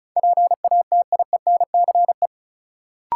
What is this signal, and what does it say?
Morse code, heard as a keyed tone, about 36 wpm, 700 Hz. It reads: PATIENCE
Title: Sentences from Top 1300 Words - New Word Drill Alt - 35wpm